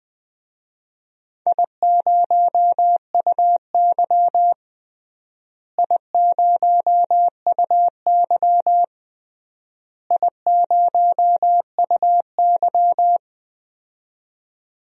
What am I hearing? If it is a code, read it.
Morse code, heard as a keyed tone, about 20 wpm, 700 Hz. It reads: I0UY I0UY I0UY
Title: DXCC-100 Call Signs - 4 Characters - Repeated 3x - 20wpm